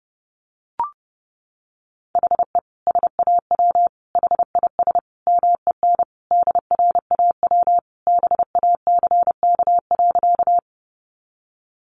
Morse code, heard as a keyed tone, about 30 wpm, 700 Hz. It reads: HE SAW HIS MEN DRAW BACK.
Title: Sentences from Top 300 Words 30wpm